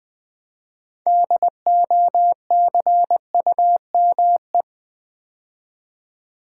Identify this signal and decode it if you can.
Morse code, heard as a keyed tone, about 21 wpm, 700 Hz. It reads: DOCUME
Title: Build A Word - 2 Letters At A Time - 8-Letter Words - 20wpm